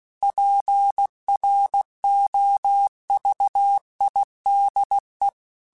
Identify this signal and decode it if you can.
Morse code, heard as a keyed tone, about 16 wpm, 800 Hz. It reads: PROVIDE
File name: Technical Subjects / Technical Subjects-mp3s / Sw Pwr Sup @15